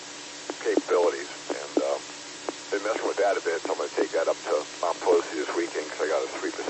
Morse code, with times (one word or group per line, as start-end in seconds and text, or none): none